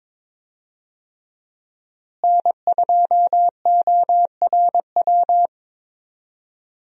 Morse code, heard as a keyed tone, about 22 wpm, 700 Hz. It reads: N2ORW
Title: USA Call Signs - 5 Characters - Repeated 3x - 22wpm